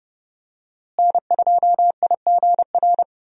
Morse code, heard as a keyed tone, about 31 wpm, 700 Hz. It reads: N2IGR
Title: USA Call Signs - 5 Characters - Repeated 2x - 30wpm